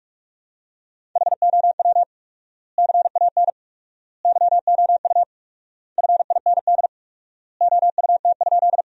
Morse code, heard as a keyed tone, about 42 wpm, 700 Hz. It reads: HOW CAN YOU FIND OUT?
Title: 4x Word-Spacing - Sentences from Top 200 Words - 45wpm